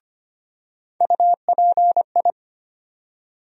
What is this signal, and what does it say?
Morse code, heard as a keyed tone, about 25 wpm, 700 Hz. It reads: UPI
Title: USA Call Sign Postfixes - 3 Letters 25wpm